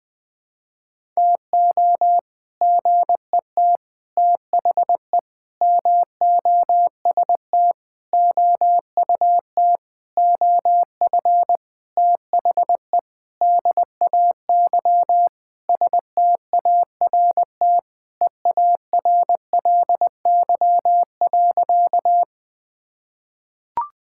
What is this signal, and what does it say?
Morse code, heard as a keyed tone, about 21 wpm, 700 Hz. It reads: TO GET THE MOST OUT OF THE DAY START EARLY.
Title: Sentences with 7-Word Phrases - 20wpm